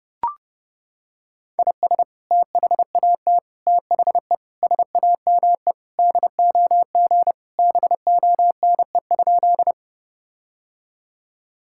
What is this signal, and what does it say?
Morse code, heard as a keyed tone, about 31 wpm, 700 Hz. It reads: IS THAT THE SAME DOG BONE?
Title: Sentences from Top 1600 Words 30wpm